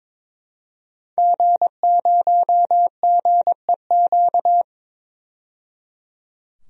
Morse code, heard as a keyed tone, about 22 wpm, 700 Hz. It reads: G0GEQ
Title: DXCC-100 Call Signs - 5 Characters - 22wpm